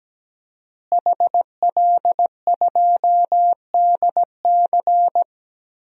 Morse code, heard as a keyed tone, about 17 wpm, 700 Hz. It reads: HL2DC
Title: DXCC-100 Call Signs - 5 Characters - Repeated 2x - 17wpm